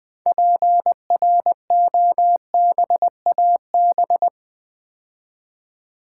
Morse code, heard as a keyed tone, about 20 wpm, 700 Hz. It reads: PROBAB
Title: Build A Word - 2 Letters At A Time - 8-Letter Words - 20wpm